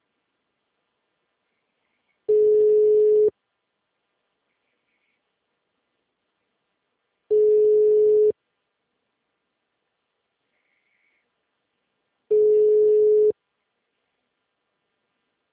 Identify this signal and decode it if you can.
Ringback tone (Europe, 425 Hz).
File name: out- -2005- - - .23657